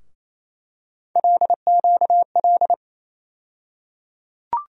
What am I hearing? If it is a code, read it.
Morse code, heard as a keyed tone, about 28 wpm, 700 Hz. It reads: LQL
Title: USA Call Sign Postfixes - 3 Letters 28wpm